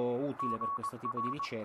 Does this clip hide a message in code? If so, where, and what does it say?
Yes: Morse code from 0.4 to 1.5 s, heard as a keyed tone, about 21 wpm, 1120 Hz, with speech. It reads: GR